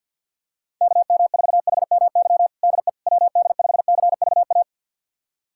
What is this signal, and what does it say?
Morse code, heard as a keyed tone, about 47 wpm, 700 Hz. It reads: KM4HMY DE WD5CVA
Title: USA Callsign DE Callsign - 6 Characters - 50wpm